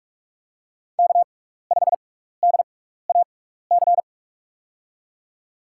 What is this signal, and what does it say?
Morse code, heard as a keyed tone, about 44 wpm, spaced as for 15 wpm, 700 Hz. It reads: K5DAC
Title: USA Call Signs - 5 Characters - 6x Character Spacing - 45wpm